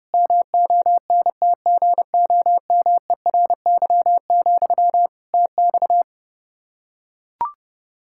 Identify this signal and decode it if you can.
Morse code, heard as a keyed tone, about 30 wpm, 700 Hz. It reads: MONTGOMERY, TX
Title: Practice QSO - Exchange 1 - Beginning - 30wpm